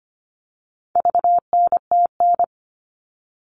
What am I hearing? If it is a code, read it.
Morse code, heard as a keyed tone, about 25 wpm, 700 Hz. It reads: VNTN
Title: All Cut Numbers - 4-Digit 25wpm